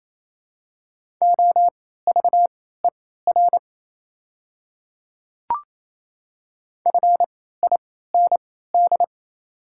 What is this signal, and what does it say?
Morse code, heard as a keyed tone, about 28 wpm, spaced as for 16 wpm, 700 Hz. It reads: OVER FIND
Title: 3x Character Spacing - Top 100 Words - 28wpm